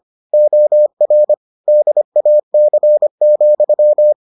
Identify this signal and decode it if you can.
Morse code, heard as a keyed tone, about 25 wpm, 600 Hz. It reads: OR DAC,